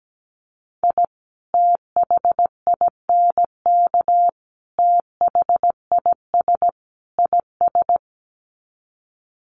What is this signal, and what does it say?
Morse code, heard as a keyed tone, about 17 wpm, 700 Hz. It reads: I THINK THIS IS
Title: Sets of 4 Words 17wpm